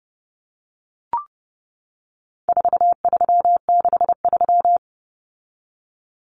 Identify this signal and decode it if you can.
Morse code, heard as a keyed tone, about 30 wpm, 700 Hz. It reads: 4363
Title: Common Cut Numbers - 4-Digit 30wpm